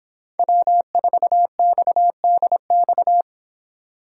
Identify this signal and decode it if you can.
Morse code, heard as a keyed tone, about 26 wpm, 700 Hz. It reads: W4XDX